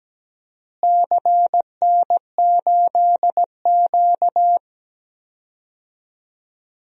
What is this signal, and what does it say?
Morse code, heard as a keyed tone, about 17 wpm, 700 Hz. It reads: CN8Q
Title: DXCC-100 Call Signs - 4 Characters - Repeated 2x - 17wpm